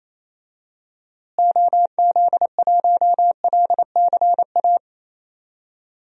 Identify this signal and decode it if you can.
Morse code, heard as a keyed tone, about 28 wpm, 700 Hz. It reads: OZ1LCA
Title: DXCC-100 Call Signs - 6 Characters - Repeated 2x - 28wpm